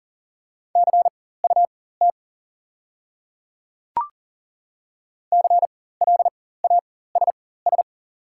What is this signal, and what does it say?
Morse code, heard as a keyed tone, about 41 wpm, spaced as for 19 wpm, 690 Hz. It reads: CUT CLASS
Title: 4x Character Spacing - Top 500 Words - 40wpm